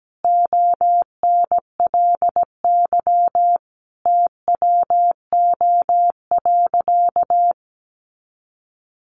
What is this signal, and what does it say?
Morse code, heard as a keyed tone, about 17 wpm, 700 Hz. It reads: ONLY TWO.
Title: Sentences from Top 200 Words 17wpm